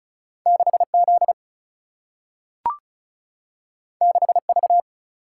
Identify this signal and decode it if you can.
Morse code, heard as a keyed tone, about 36 wpm, 700 Hz. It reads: 6 Z 6 V
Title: Single-Letter Single-Number 35wpm